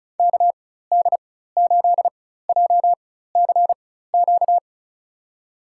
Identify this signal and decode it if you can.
Morse code, heard as a keyed tone, about 34 wpm, spaced as for 16 wpm, 700 Hz. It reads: KD8JCQ
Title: USA Call Signs - 6 Characters - 4x Character Spacing - 35wpm